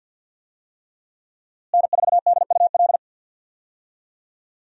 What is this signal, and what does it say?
Morse code, heard as a keyed tone, about 52 wpm, 700 Hz. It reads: N4DUL
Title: USA Call Signs - 5 Characters - Repeated 3x - 50wpm